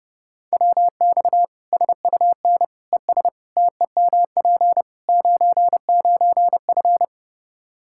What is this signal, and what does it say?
Morse code, heard as a keyed tone, about 32 wpm, 700 Hz. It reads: WX SUN ES TEMP 99F
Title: Practice QSO - Exchange 2 - Info - 30wpm